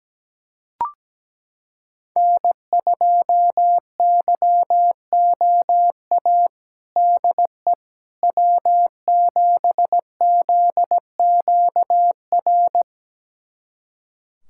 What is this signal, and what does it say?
Morse code, heard as a keyed tone, about 17 wpm, 700 Hz. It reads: N2YOA DE W7ZQR
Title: USA Callsign DE Callsign - 5 Characters - 17wpm